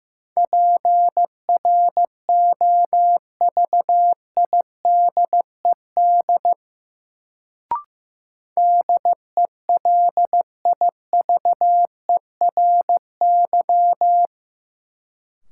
Morse code, heard as a keyed tone, about 15 wpm, 700 Hz. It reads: PROVIDED DELIVERY
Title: Sentences from Top 1500 Words - New Word Drill - 15wpm